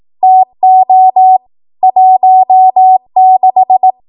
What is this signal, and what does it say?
Morse code, heard as a keyed tone, about 18 wpm, 750 Hz. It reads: TO 16
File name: Morse / 18WPM